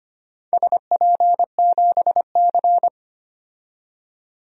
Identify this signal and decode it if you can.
Morse code, heard as a keyed tone, about 25 wpm, 700 Hz. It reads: SP7C
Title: DXCC-100 Call Signs - 4 Characters - 25wpm